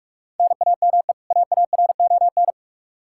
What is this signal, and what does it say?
Morse code, heard as a keyed tone, about 42 wpm, 700 Hz. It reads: NAME AARON
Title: Practice QSO - Exchange 1 - Beginning - 45wpm